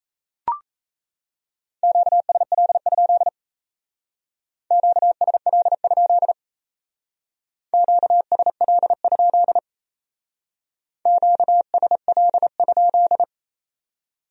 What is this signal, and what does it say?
Morse code, heard as a keyed tone, about 36 wpm, 700 Hz. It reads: QSL? QSL? QSL? QSL?